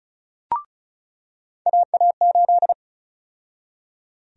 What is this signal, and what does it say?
Morse code, heard as a keyed tone, about 36 wpm, 700 Hz. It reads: AA8
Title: USA Call Sign Prefixes - 35wpm